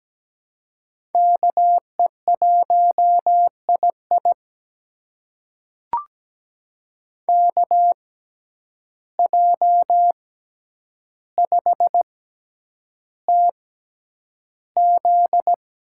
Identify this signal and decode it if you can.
Morse code, heard as a keyed tone, about 17 wpm, spaced as for 9 wpm, 700 Hz. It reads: KE1II KJ5TZ
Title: USA Call Signs - 5 Characters - 6x Character Spacing - 17wpm